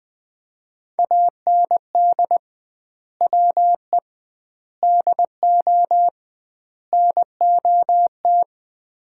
Morse code, heard as a keyed tone, about 21 wpm, 700 Hz. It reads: AND WE DO NOT
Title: 2x Word-Spacing - Sets of 4 Words - 20wpm